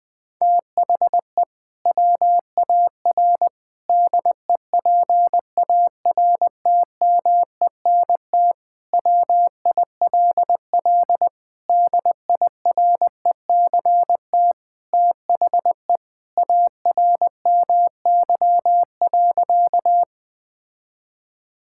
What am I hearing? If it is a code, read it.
Morse code, heard as a keyed tone, about 19 wpm, 700 Hz. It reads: THE WAR DEPARTMENT WILL DIRECT THE ARMY.
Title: Sentences from Top 1000 Words - Review Part 5 - 20wpm